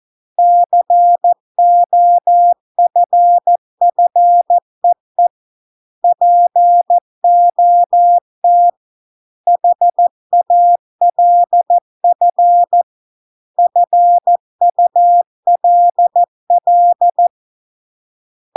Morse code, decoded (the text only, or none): COFFEE POT HALF FULL